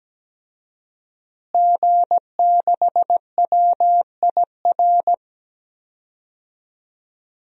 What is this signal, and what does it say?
Morse code, heard as a keyed tone, about 17 wpm, 700 Hz. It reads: G6WIR
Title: DXCC-100 Call Signs - 5 Characters - Repeated 2x - 17wpm